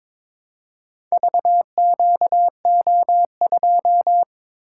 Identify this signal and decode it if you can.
Morse code, heard as a keyed tone, about 22 wpm, 700 Hz. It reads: VQO2
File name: practice / 22wpm / Four Letter-Number Combinations - 22wpm